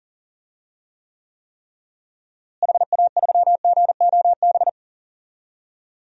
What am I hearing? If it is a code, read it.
Morse code, heard as a keyed tone, about 38 wpm, 700 Hz. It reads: HA3GOB
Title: DXCC-100 Call Signs - 6 Characters - Repeated 3x - 40wpm